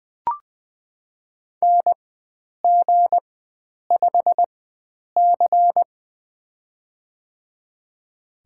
Morse code, heard as a keyed tone, about 21 wpm, spaced as for 9 wpm, 700 Hz. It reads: NG5C
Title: USA Call Signs - 4x Character Spacing - 20wpm